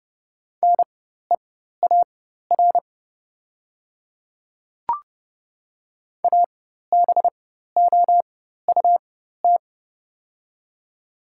Morse code, heard as a keyed tone, about 30 wpm, spaced as for 14 wpm, 700 Hz. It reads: NEAR ABOUT
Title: 4x Character Spacing - Top 200 Words - 30wpm